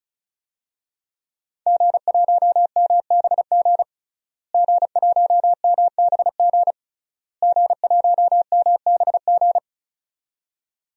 Morse code, heard as a keyed tone, about 36 wpm, 690 Hz. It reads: G1MBG G1MBG G1MBG
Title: DXCC-100 Call Signs - 5 Characters - Repeated 3x - 35wpm